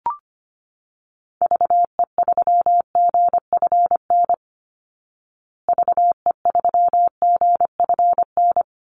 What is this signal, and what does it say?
Morse code, heard as a keyed tone, about 25 wpm, 700 Hz. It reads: VE3GFN VE3GFN
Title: Call Signs Repeated 2x - 25wpm